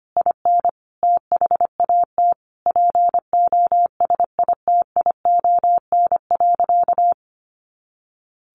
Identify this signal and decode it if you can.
Morse code, heard as a keyed tone, about 25 wpm, 700 Hz. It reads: IN THAT POSITION.
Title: Sentences from Top 900 Words 25wpm